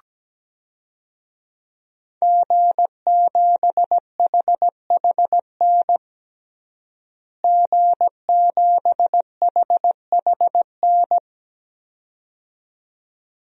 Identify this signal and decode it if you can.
Morse code, heard as a keyed tone, about 17 wpm, 700 Hz. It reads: G7HHN G7HHN
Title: DXCC-100 Call Signs - 5 Characters - Repeated 2x - 17wpm